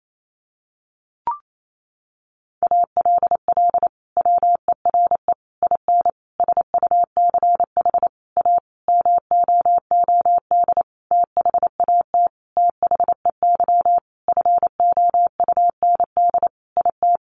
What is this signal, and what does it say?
Morse code, heard as a keyed tone, about 28 wpm, 700 Hz. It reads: ALL WERE IN SUCH A MOOD THAT THEY FOUND IT
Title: Sentences from Top 1100 Words 28wpm